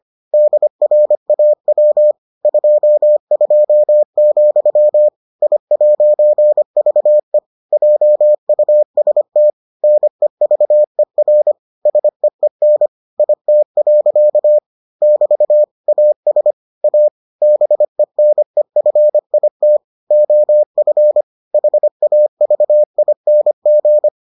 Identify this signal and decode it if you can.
Morse code, heard as a keyed tone, about 25 wpm, 600 Hz. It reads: DRAW 22, I'VE JUST NEVER SEEN IT. <BT> AS A BENEFIT OF HAVING